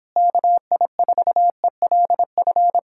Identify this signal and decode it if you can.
Morse code, heard as a keyed tone, about 26 wpm, 700 Hz. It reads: KI4ELF